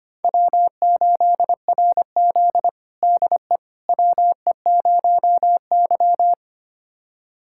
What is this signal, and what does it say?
Morse code, heard as a keyed tone, about 25 wpm, 700 Hz. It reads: W8RZ DE WE0Y
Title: USA Callsign DE Callsign - 4 Characters - 25wpm